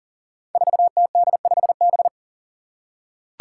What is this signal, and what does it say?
Morse code, heard as a keyed tone, about 38 wpm, 700 Hz. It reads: 4TD5B
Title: Five Challenging Letter-Number Combinations 40wpm